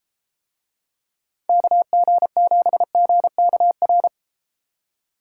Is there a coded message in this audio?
Yes: Morse code from 1.5 to 4.1 s, heard as a keyed tone, about 32 wpm, 700 Hz. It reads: KG7GKR